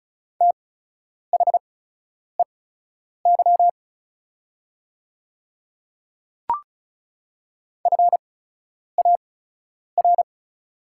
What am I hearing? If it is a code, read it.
Morse code, heard as a keyed tone, about 36 wpm, spaced as for 10 wpm, 700 Hz. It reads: THEY FAR
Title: 8x Character Spacing - Top 300 Words - 35wpm